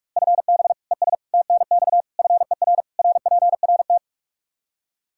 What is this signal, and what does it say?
Morse code, heard as a keyed tone, about 41 wpm, 700 Hz. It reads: FB ES TNX FER RPRT
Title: Practice QSO - Exchange 2 - Info - 45wpm